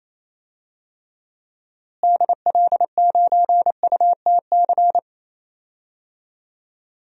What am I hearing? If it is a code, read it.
Morse code, heard as a keyed tone, about 28 wpm, 700 Hz. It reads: DL9UTC
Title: DXCC-100 Call Signs - 6 Characters - Repeated 3x - 28wpm